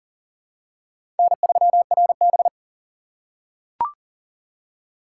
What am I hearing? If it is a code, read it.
Morse code, heard as a keyed tone, about 40 wpm, 700 Hz. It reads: N3RB
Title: USA Call Signs - 4 Characters - 6x Character Spacing - 40wpm